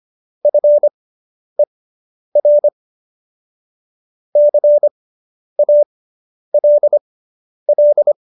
Morse code, heard as a keyed tone, about 25 wpm, spaced as for 10 wpm, 590 Hz. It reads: FER CALL